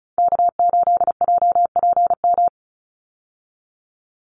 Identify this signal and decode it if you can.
Morse code, heard as a keyed tone, about 34 wpm, 700 Hz. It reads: K8JPM
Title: Call Signs - 35wpm